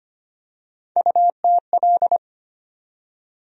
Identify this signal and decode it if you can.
Morse code, heard as a keyed tone, about 25 wpm, 700 Hz. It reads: UTL
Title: USA Call Sign Postfixes - 3 Letters 25wpm